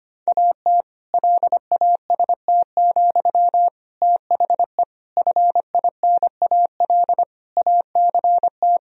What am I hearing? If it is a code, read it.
Morse code, heard as a keyed tone, about 25 wpm, 700 Hz. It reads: AT LAST, THE FINAL ACT
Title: Sentences from Top 1000 Words - Review Part 2 - 25wpm